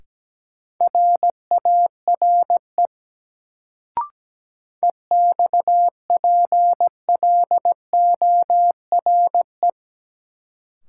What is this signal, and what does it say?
Morse code, heard as a keyed tone, about 17 wpm, 700 Hz. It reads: RARE EXPLORE
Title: Sentences from Top 1100 Words - New Word Drill - 17wpm